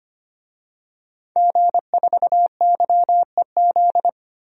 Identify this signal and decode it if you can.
Morse code, heard as a keyed tone, about 25 wpm, 700 Hz. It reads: G4YEZ
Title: DXCC-100 Call Signs - 5 Characters - Repeated 3x - 25wpm